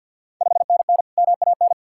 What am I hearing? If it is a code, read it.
Morse code, heard as a keyed tone, about 51 wpm, 690 Hz. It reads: 5NN MAN
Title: Ontario QSO Party Exchange 50wpm